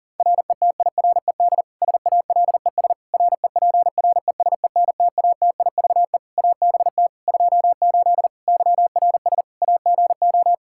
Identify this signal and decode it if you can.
Morse code, heard as a keyed tone, about 40 wpm, 700 Hz. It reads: RETIRED SALES REPRESENTATIVE ABT 28 YRS AGO
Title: Practice QSO - Exchange 3 - More Info - 40wpm